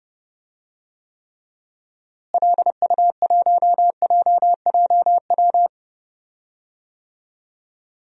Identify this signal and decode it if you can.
Morse code, heard as a keyed tone, about 30 wpm, 700 Hz. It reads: LU1JJW
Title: DXCC-100 Call Signs - 6 Characters - Repeated 3x - 30wpm